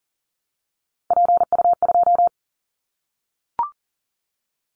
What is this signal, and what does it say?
Morse code, heard as a keyed tone, about 38 wpm, 700 Hz. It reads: PU2
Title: Callsign Prefixes for Top 100 DXCC - 40wpm